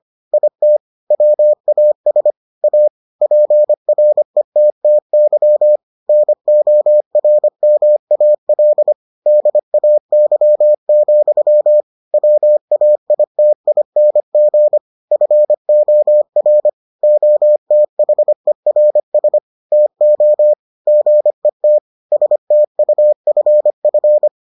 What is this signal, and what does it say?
Morse code, heard as a keyed tone, about 25 wpm, 600 Hz. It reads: IT WAS A PRETTY NORMAL DAY, WAITING FOR OTHERS TO GET STUFF